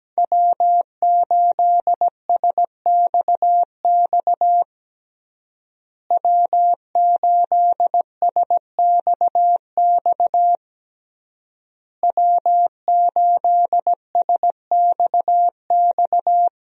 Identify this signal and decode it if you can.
Morse code, heard as a keyed tone, about 17 wpm, 700 Hz. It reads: W8SXX W8SXX W8SXX
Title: USA Call Signs - 5 Characters - Repeated 3x - 17wpm